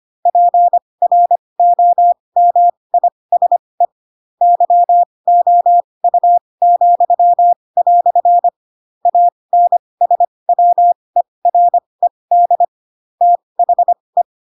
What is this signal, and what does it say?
Morse code, heard as a keyed tone, about 25 wpm, spaced as for 20 wpm, 700 Hz. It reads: PROMISE YOU," ANSWERED THE
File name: Pinocchio / Pinocchio ch8 15 25 ik1hge